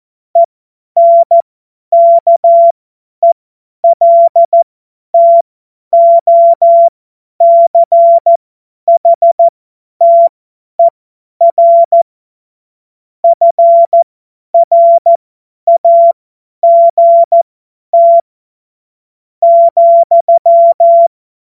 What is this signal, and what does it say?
Morse code, heard as a keyed tone, about 14 wpm, spaced as for 10 wpm, 680 Hz. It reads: ENKELTOCHTER FRAGT ,